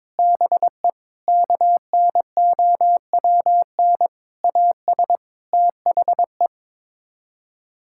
Morse code, heard as a keyed tone, about 22 wpm, 700 Hz. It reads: BE KNOWN AS THE
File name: practice / 22wpm / Sets of 8 Words - 22wpm